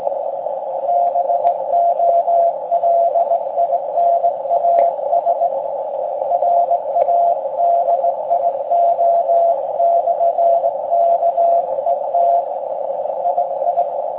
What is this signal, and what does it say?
Morse code, heard as a keyed tone, about 19 wpm, 690 Hz, with noise. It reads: BOLINAS RADIO/KET II